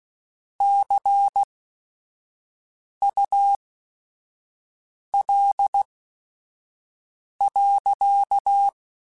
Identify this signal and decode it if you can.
Morse code, heard as a keyed tone, about 16 wpm, spaced as for 5 wpm, 800 Hz. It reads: CUL.